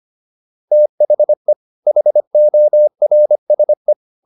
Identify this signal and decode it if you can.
Morse code, heard as a keyed tone, about 25 wpm, 600 Hz. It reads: THE HORSE